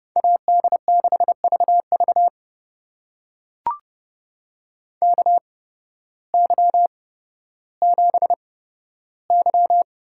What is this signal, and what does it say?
Morse code, heard as a keyed tone, about 32 wpm, spaced as for 14 wpm, 700 Hz. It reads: AD6VV KY7Y
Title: USA Call Signs - 8x Character Spacing - 30wpm